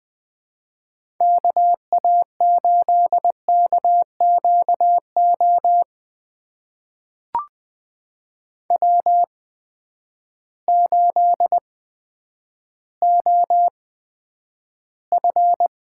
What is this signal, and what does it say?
Morse code, heard as a keyed tone, about 20 wpm, spaced as for 10 wpm, 700 Hz. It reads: KA8KQO W8OF
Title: USA Call Signs - 8x Character Spacing - 20wpm